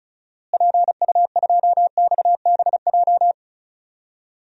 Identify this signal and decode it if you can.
Morse code, heard as a keyed tone, about 36 wpm, 700 Hz. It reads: PU2XBJ
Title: DXCC-100 Call Signs - 6 Characters - Repeated 2x - 35wpm